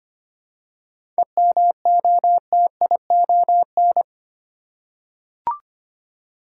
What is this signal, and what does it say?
Morse code, heard as a keyed tone, about 25 wpm, 700 Hz. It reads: EMOTION
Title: Seven Letter Words - 25wpm